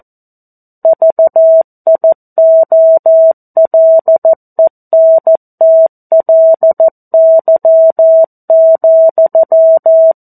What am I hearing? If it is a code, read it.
Morse code, heard as a keyed tone, about 14 wpm, 650 Hz. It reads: VIOLENTLY,
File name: 02-Chapter-02 14 WPM